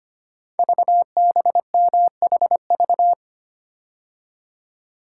Morse code, heard as a keyed tone, about 25 wpm, 700 Hz. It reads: VBMHV
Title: Code Groups - 5 Letters - 25wpm